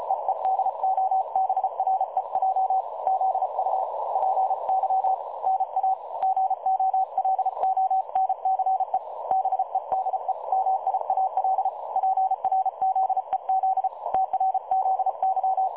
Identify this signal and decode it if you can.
Morse code, heard as a keyed tone, about 35 wpm, 760 Hz. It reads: EE DOBREJ ZUPY NA GOLONCE BEDE MIAL PRZEZ TRZY
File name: dlazaawansowanych / nagraniazpasma / websdr recording 3565.0kHz